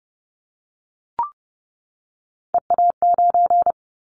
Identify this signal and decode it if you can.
Morse code, heard as a keyed tone, about 30 wpm, 700 Hz. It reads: EA9
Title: Callsign Prefixes for Top 100 DXCC - 30wpm